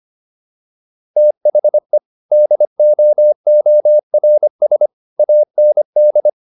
Morse code, heard as a keyed tone, about 25 wpm, 600 Hz. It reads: THE DOORS AND